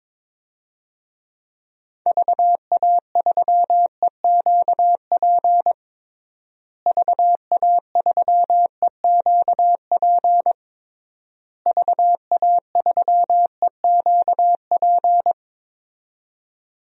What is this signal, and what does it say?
Morse code, heard as a keyed tone, about 22 wpm, 700 Hz. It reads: VA3EQP VA3EQP VA3EQP
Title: DXCC-100 Call Signs - 6 Characters - Repeated 3x - 22wpm